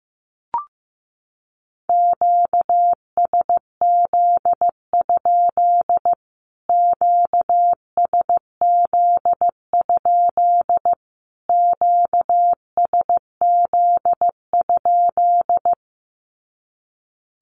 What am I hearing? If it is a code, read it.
Morse code, heard as a keyed tone, about 15 wpm, 700 Hz. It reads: QSZ? QSZ? QSZ?
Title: Q-Codes Repeated 3x 15wpm